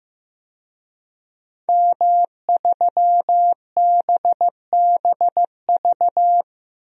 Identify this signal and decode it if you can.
Morse code, heard as a keyed tone, about 15 wpm, 700 Hz. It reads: M3BBV
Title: DXCC-100 Call Signs - 5 Characters - 15wpm